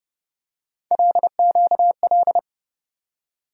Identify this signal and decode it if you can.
Morse code, heard as a keyed tone, about 32 wpm, 700 Hz. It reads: LQL
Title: USA Call Sign Postfixes - 3 Letters 30wpm